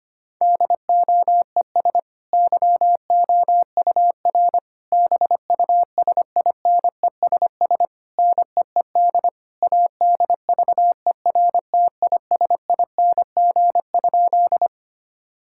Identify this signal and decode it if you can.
Morse code, heard as a keyed tone, about 25 wpm, 700 Hz. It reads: DOES YOUR BUSINESS NEED ADVERTISING?
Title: Sentences from Top 1000 Words - Review Part 6 - 25wpm